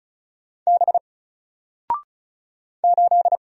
Morse code, heard as a keyed tone, about 35 wpm, 700 Hz. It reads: B 8
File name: Single-Letter-Number-35wpm